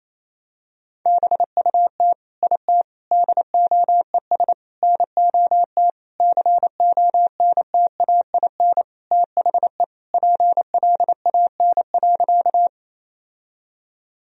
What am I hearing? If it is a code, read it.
Morse code, heard as a keyed tone, about 28 wpm, 700 Hz. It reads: BUT IT DOES NOT CONTAIN THE PLAN.
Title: Sentences from Top 1000 Words - Review Part 2 - 28wpm